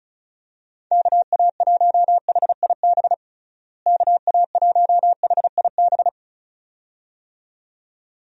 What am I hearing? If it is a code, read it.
Morse code, heard as a keyed tone, about 36 wpm, 700 Hz. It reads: KA1HIB KA1HIB
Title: USA Call Signs - 6 Characters - Repeated 2x - 35wpm